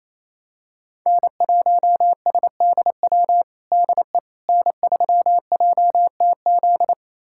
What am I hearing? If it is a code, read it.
Morse code, heard as a keyed tone, about 28 wpm, 700 Hz. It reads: N1SDW DE N3JTZ
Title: USA Callsign DE Callsign - 5 Characters - 28wpm